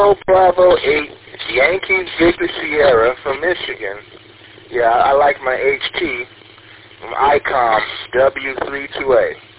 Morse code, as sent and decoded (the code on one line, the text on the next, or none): none